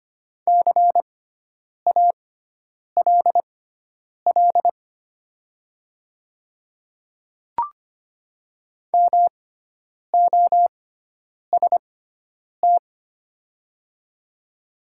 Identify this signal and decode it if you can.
Morse code, heard as a keyed tone, about 25 wpm, spaced as for 9 wpm, 700 Hz. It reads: CALL MOST
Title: 6x Character Spacing - Top 100 Words - 25wpm